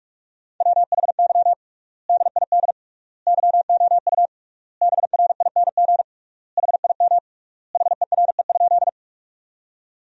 Morse code, heard as a keyed tone, about 46 wpm, 700 Hz. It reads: WHY DID YOU BRING HIM HERE?
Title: 3x Word-Spacing - Sentences from Top 500 Words - 45wpm